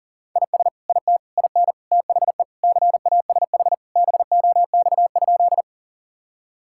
Morse code, heard as a keyed tone, about 40 wpm, 700 Hz. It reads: IS IT IN THE CASH BOX?
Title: Sentences from Top 1000 Words - Review Part 5 - 40wpm